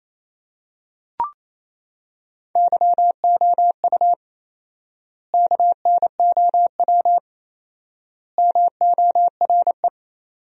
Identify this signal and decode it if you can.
Morse code, heard as a keyed tone, about 28 wpm, 700 Hz. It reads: YOU KNOW MORE
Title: 4x Word-Spacing - Sentences from Top 100 Words - 28wpm